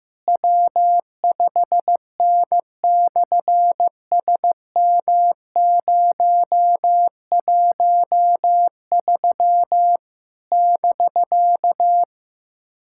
Morse code, heard as a keyed tone, about 15 wpm, 700 Hz. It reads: W5N/SM013 <BK>
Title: SOTA QSO Practice 15wpm